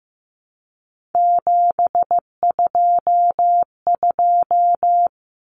Morse code, heard as a keyed tone, about 15 wpm, 700 Hz. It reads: 722
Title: Common Cut Numbers - 3-Digit 15wpm